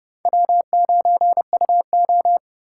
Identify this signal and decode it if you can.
Morse code, heard as a keyed tone, about 28 wpm, 700 Hz. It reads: W9UO